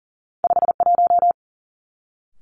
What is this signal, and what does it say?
Morse code, heard as a keyed tone, about 43 wpm, 700 Hz. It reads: 5 1
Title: Two Challenging Letter-Number Combinations 40wpm